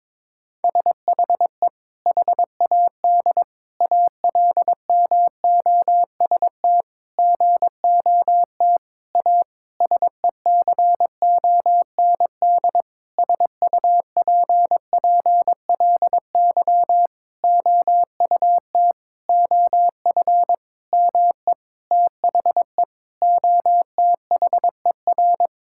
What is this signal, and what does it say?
Morse code, heard as a keyed tone, about 22 wpm, 700 Hz. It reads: SHE HAD ALMOST GOT A SECOND SUPPLY OUT OF ME THE OTHER
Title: Sentences from Top 1000 Words - Review Part 5 - 22wpm